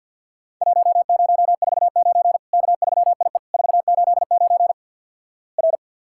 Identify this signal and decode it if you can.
Morse code, heard as a keyed tone, about 50 wpm, 700 Hz. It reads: 1049 K3IE 489 R
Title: Worked All Europe DX Contest - QTC Exchange - 50wpm